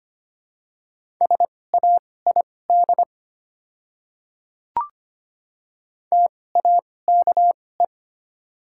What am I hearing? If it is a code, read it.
Morse code, heard as a keyed tone, about 25 wpm, spaced as for 18 wpm, 700 Hz. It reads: SAID TAKE
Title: 2x Character Spacing - Top 200 Words - 25wpm